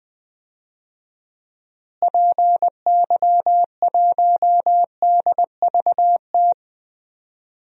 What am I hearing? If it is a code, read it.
Morse code, heard as a keyed tone, about 20 wpm, 700 Hz. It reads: PY1DVT
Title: DXCC-100 Call Signs - 6 Characters - Repeated 3x - 20wpm